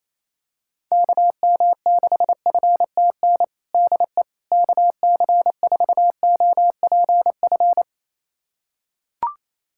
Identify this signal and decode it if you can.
Morse code, heard as a keyed tone, about 28 wpm, 700 Hz. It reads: KM6FTN DE KC4OPF
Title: USA Callsign DE Callsign 28wpm